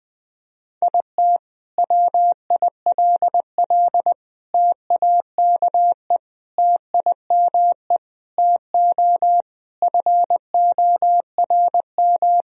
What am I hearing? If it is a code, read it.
Morse code, heard as a keyed tone, about 21 wpm, 700 Hz. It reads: IT WILL TAKE TIME TO FORM
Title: Sentences from Top 1000 Words - Review Part 1 - 20wpm